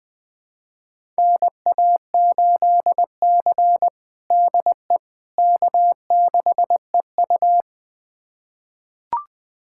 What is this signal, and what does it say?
Morse code, heard as a keyed tone, about 21 wpm, 700 Hz. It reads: NA8C DE K6EU
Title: USA Callsign DE Callsign - 4 Characters - 20wpm